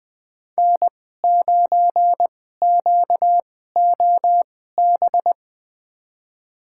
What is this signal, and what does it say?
Morse code, heard as a keyed tone, about 20 wpm, spaced as for 14 wpm, 700 Hz. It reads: N9QOB
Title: USA Call Signs - 2x Character Spacing - 20wpm